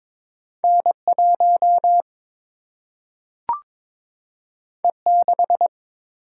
Morse code, heard as a keyed tone, about 22 wpm, 700 Hz. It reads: N1 E6
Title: Single-Letter Single-Number 22wpm